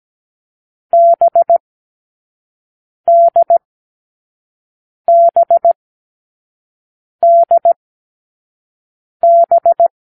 Morse code, heard as a keyed tone, about 17 wpm, spaced as for 5 wpm, 680 Hz. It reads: BDBDB
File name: k7qo / Lesson6(A-D)